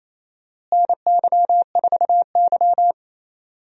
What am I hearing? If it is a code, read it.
Morse code, heard as a keyed tone, about 27 wpm, 700 Hz. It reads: NY4Y
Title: USA Call Signs - 4 Characters - Repeated 2x - 28wpm